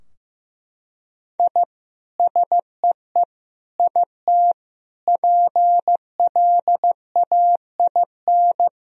Morse code, heard as a keyed tone, about 15 wpm, 700 Hz. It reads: I SEE IT PLAIN
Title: Sentences from Top 1000 Words - Review Part 2 - 15wpm